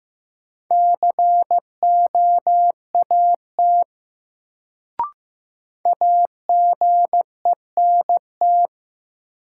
Morse code, heard as a keyed tone, about 15 wpm, 700 Hz. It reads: COAT AGENT
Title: Sentences from Top 1600 Words - New Word Drill - 15wpm